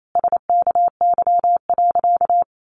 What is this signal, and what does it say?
Morse code, heard as a keyed tone, about 28 wpm, 700 Hz. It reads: SKY.
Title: A Sound Of Thunder 28wpm